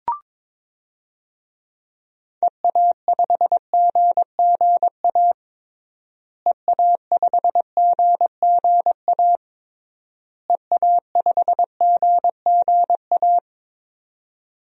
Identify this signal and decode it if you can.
Morse code, heard as a keyed tone, about 22 wpm, 700 Hz. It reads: EA5GGA EA5GGA EA5GGA
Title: DXCC-100 Call Signs - 6 Characters - Repeated 3x - 22wpm